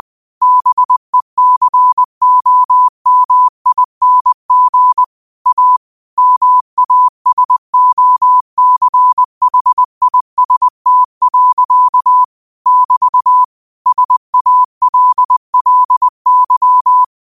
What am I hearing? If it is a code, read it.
Morse code, heard as a keyed tone, about 20 wpm, 1000 Hz. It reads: BECOMING A MASOCHIST. <BT> SALLY
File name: qotdfiles / 20wpm